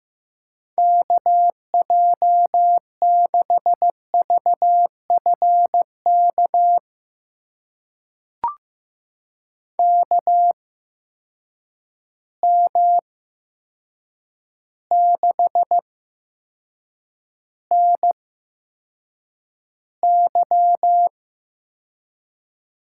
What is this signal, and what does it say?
Morse code, heard as a keyed tone, about 15 wpm, spaced as for 7 wpm, 700 Hz. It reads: KJ6VFK KM6NY
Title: USA Call Signs - 6 Characters - 8x Character Spacing - 15wpm